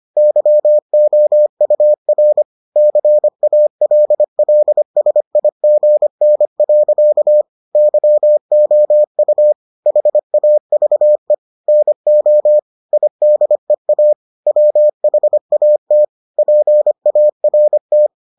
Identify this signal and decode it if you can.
Morse code, heard as a keyed tone, about 25 wpm, 600 Hz. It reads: YOUR CALLSIGN. YOU HAVE NO IDEA WHAT PART